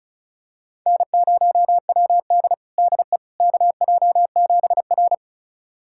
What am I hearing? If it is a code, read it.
Morse code, heard as a keyed tone, about 34 wpm, 700 Hz. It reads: N0WD DE KJ7R
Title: USA Callsign DE Callsign - 4 Characters - 35wpm